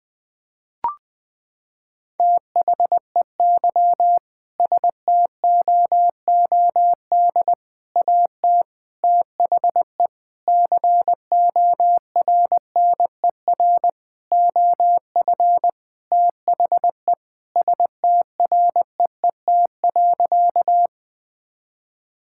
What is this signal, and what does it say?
Morse code, heard as a keyed tone, about 20 wpm, 700 Hz. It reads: THEY STOOD AT THE CORNER OF THE STREET.
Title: Sentences from Top 1600 Words 20wpm